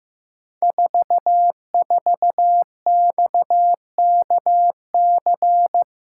Morse code, heard as a keyed tone, about 15 wpm, 700 Hz. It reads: 44XKC
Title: Code Groups - 5 Letter:Numbers - 15wpm